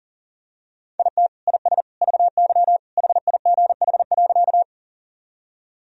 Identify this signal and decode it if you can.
Morse code, heard as a keyed tone, about 37 wpm, 700 Hz. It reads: IT IS VY HIGH.
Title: Sentences from Top 200 Words and Abbreviations 40wpm